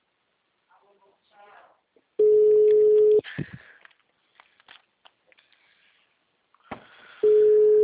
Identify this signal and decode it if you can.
Ringback tone (Europe, 425 Hz), with speech.